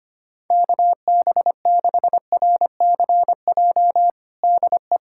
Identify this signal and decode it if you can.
Morse code, heard as a keyed tone, about 25 wpm, 700 Hz. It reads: KB6RCJ DE
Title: USA Callsign DE Callsign - 6 Characters - 25wpm